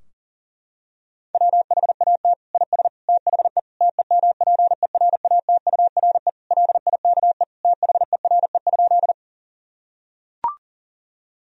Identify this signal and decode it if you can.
Morse code, heard as a keyed tone, about 40 wpm, 700 Hz. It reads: WHAT IS THE TEMPERATURE LIKE THERE?
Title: Sentences from Top 1000 Words - Review Part 4 - 40wpm